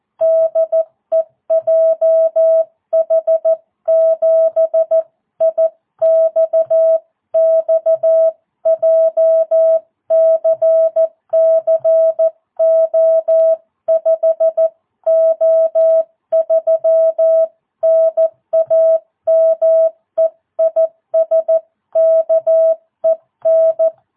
Morse code, heard as a keyed tone, about 14 wpm, 650 Hz. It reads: DE JH7IXX JCC O5O3 NAME IS KEN